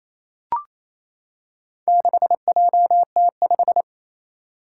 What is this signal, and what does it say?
Morse code, heard as a keyed tone, about 28 wpm, 700 Hz. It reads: 6JT5
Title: Four Challenging Letter-Number Combinations 28wpm